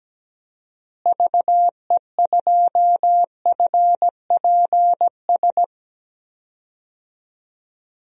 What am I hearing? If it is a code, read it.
Morse code, heard as a keyed tone, about 17 wpm, 700 Hz. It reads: VE2FPS
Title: DXCC-100 Call Signs - 6 Characters - Repeated 3x - 17wpm